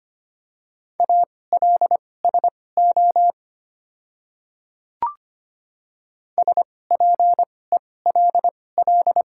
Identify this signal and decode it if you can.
Morse code, heard as a keyed tone, about 25 wpm, spaced as for 18 wpm, 690 Hz. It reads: ALSO SPELL
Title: 2x Character Spacing - Top 300 Words - 25wpm